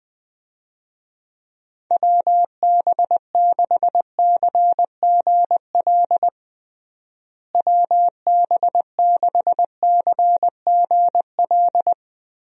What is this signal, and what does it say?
Morse code, heard as a keyed tone, about 21 wpm, 700 Hz. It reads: WB6CGL WB6CGL
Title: USA Call Signs - 6 Characters - Repeated 2x - 20wpm